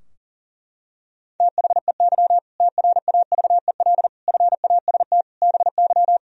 Morse code, heard as a keyed tone, about 37 wpm, 700 Hz. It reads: THEY TRAVEL FAST BY
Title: Sentences from Top 1000 Words - Review Part 3 - 40wpm